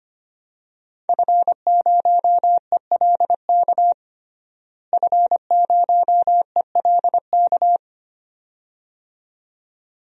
Morse code, heard as a keyed tone, about 25 wpm, 700 Hz. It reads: F0ELK F0ELK
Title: DXCC-100 Call Signs - 5 Characters - Repeated 2x - 25wpm